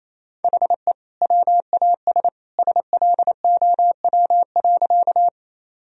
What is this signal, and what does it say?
Morse code, heard as a keyed tone, about 28 wpm, 700 Hz. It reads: HE WAS SLOW.
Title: Sentences from Top 1000 Words - Review Part 2 - 28wpm